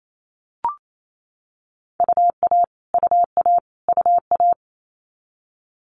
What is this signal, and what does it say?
Morse code, heard as a keyed tone, about 28 wpm, 700 Hz. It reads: UA UA UA
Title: Multi-letter Phonemes Repeated 3x 28wpm